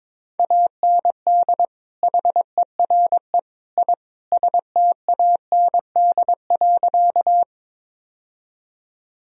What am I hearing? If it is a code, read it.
Morse code, heard as a keyed tone, about 22 wpm, 700 Hz. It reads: AND HERE I STAND.
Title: Sentences from Top 1000 Words - Review Part 1 - 22wpm